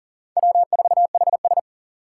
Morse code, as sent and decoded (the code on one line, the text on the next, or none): .-- ....- .... ...
W4HS